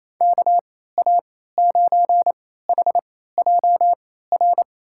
Morse code, heard as a keyed tone, about 27 wpm, spaced as for 16 wpm, 700 Hz. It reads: KA9HJR
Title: USA Call Signs - 6 Characters - 3x Character Spacing - 28wpm